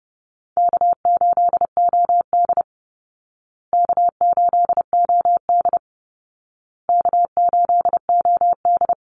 Morse code, heard as a keyed tone, about 30 wpm, 700 Hz. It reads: K8OD K8OD K8OD
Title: Easy Call Signs Repeated 3x - 30wpm